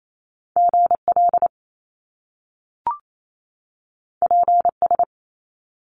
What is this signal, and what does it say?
Morse code, heard as a keyed tone, about 28 wpm, 700 Hz. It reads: GL PS
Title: Top Two Letter Combinations in Words 28wpm